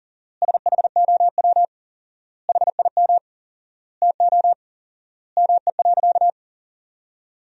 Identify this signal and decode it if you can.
Morse code, heard as a keyed tone, about 40 wpm, 700 Hz. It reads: SHOW HIM TO ME.
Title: 4x Word-Spacing - Sentences from Top 200 Words - 40wpm